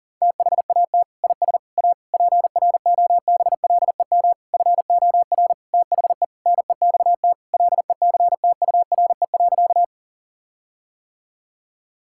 Morse code, heard as a keyed tone, about 40 wpm, 700 Hz. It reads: THAT IS A PROBLEM FOR THE NEXT LECTURE.
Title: Sentences from Top 1600 Words 40wpm